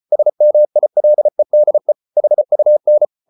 Morse code, heard as a keyed tone, about 34 wpm, 600 Hz. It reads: SMILEDE HUN
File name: Ferie QRQ 34wpm